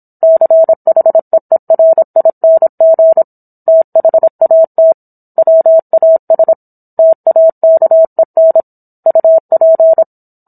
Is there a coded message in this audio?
Yes: Morse code from 0.2 to 10.0 s, heard as a keyed tone, about 27 wpm, 650 Hz. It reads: CHEERING THAT WAS TAKEN UP